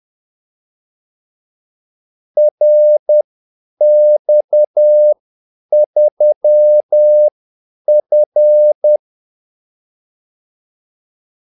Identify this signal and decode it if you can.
Morse code, heard as a keyed tone, about 10 wpm, spaced as for 8 wpm, 600 Hz. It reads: RX3F
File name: Callsigns-10-8wpm